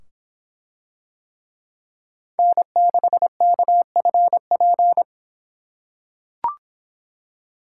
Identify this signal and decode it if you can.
Morse code, heard as a keyed tone, about 24 wpm, 700 Hz. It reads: N6KFP